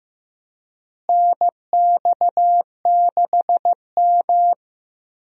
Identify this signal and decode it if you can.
Morse code, heard as a keyed tone, about 15 wpm, 700 Hz. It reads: NX6M
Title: USA Call Signs - 4 Characters - Repeated 2x - 15wpm